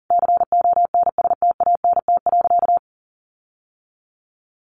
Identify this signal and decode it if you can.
Morse code, heard as a keyed tone, about 40 wpm, 700 Hz. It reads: CONSTANT.
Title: Sentences from Top 1500 Words 40wpm